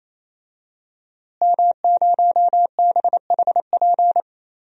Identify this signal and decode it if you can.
Morse code, heard as a keyed tone, about 27 wpm, 700 Hz. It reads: M0BHP
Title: DXCC-100 Call Signs - 5 Characters - Repeated 3x - 28wpm